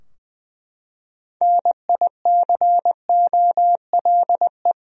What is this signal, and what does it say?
Morse code, heard as a keyed tone, about 20 wpm, 700 Hz. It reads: NICOLE
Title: USA Common Names - 20wpm